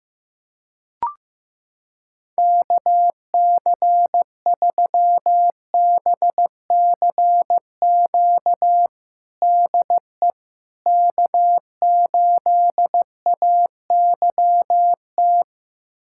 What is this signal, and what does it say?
Morse code, heard as a keyed tone, about 15 wpm, 700 Hz. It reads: KC3BCQ DE K8AYT
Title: USA Callsign DE Callsign 15wpm